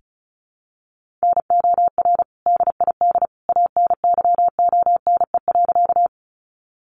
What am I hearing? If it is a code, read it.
Morse code, heard as a keyed tone, about 34 wpm, 700 Hz. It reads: NOR DID ANYONE.
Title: Sentences from Top 1500 Words 35wpm